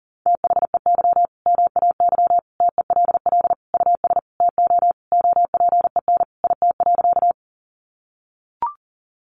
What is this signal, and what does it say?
Morse code, heard as a keyed tone, about 40 wpm, 700 Hz. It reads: THEY MAY TELL US TO OPEN IT.
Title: Sentences from Top 300 Words 40wpm